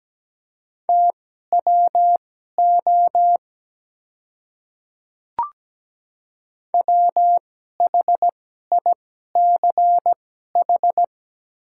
Morse code, heard as a keyed tone, about 17 wpm, spaced as for 12 wpm, 700 Hz. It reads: TWO WHICH
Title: 2x Character Spacing - Top 200 Words - 17wpm